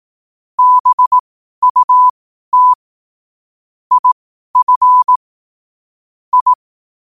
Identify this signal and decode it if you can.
Morse code, heard as a keyed tone, about 18 wpm, spaced as for 12 wpm, 1000 Hz. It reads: BUT IF I
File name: qotdfiles / 12wpm